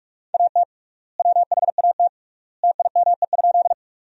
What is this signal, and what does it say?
Morse code, heard as a keyed tone, about 46 wpm, 700 Hz. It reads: AT WHAT TIME?
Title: 3x Word-Spacing - Sentences from Top 100 Words - 45wpm